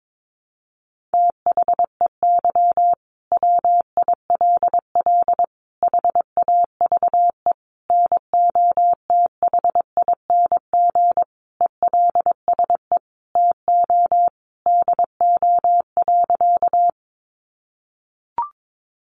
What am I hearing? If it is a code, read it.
Morse code, heard as a keyed tone, about 22 wpm, 700 Hz. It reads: THEY WILL HAVE NOTHING ELSE TO DO.
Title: Sentences from Top 800 Words 22wpm